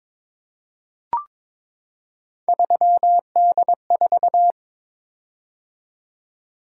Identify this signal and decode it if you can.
Morse code, heard as a keyed tone, about 23 wpm, 700 Hz. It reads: 3D4
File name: practice / 22wpm / Three Letter-Number Combinations - 22wpm